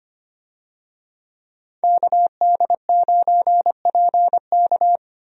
Morse code, heard as a keyed tone, about 25 wpm, 700 Hz. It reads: KD9PK
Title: USA Call Signs - 5 Characters - 25wpm